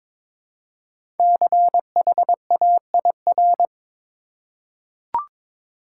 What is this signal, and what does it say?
Morse code, heard as a keyed tone, about 22 wpm, 700 Hz. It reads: CHAIR
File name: 5 Letter Words - First 3 Letters Given - 22wpm